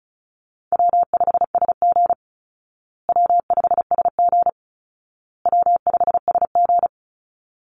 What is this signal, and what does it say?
Morse code, heard as a keyed tone, about 34 wpm, 700 Hz. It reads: W5SG W5SG W5SG
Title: Call Signs Repeated 2x - 35wpm